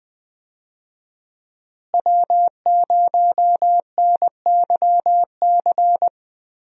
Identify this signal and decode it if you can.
Morse code, heard as a keyed tone, about 20 wpm, 700 Hz. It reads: W0NYC
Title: USA Call Signs - 5 Characters - Repeated 2x - 20wpm